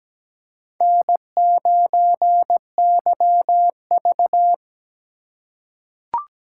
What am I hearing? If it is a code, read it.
Morse code, heard as a keyed tone, about 17 wpm, 700 Hz. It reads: N9YV
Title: USA Call Signs - 4 Characters - 3x Character Spacing - 17wpm